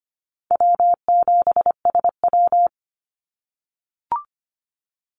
Morse code, heard as a keyed tone, about 25 wpm, 700 Hz. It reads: W7SW
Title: Easy Call Signs - 25wpm